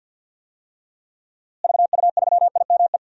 Morse code, heard as a keyed tone, about 52 wpm, 700 Hz. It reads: VU3IME
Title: DXCC-100 Call Signs - 6 Characters - Repeated 3x - 50wpm